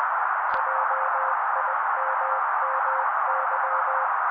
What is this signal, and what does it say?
Morse code, heard as a keyed tone, about 20 wpm, 530 Hz. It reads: JIMMY
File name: daily / 18wpm noise